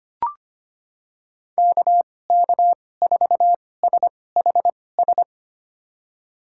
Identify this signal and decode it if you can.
Morse code, heard as a keyed tone, about 25 wpm, spaced as for 18 wpm, 700 Hz. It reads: KK4SHS
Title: USA Call Signs - 6 Characters - 2x Character Spacing - 25wpm